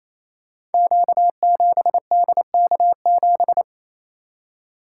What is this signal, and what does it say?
Morse code, heard as a keyed tone, about 28 wpm, 700 Hz. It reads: Q7DK7
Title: Code Groups - 5 Letter:Numbers - 28wpm